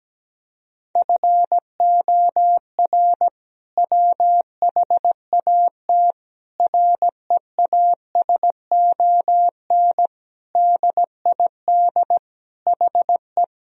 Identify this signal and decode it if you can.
Morse code, heard as a keyed tone, about 17 wpm, 700 Hz. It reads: FOR WHAT REASON DID HE
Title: Sentences from Top 1000 Words - Review Part 4 - 17wpm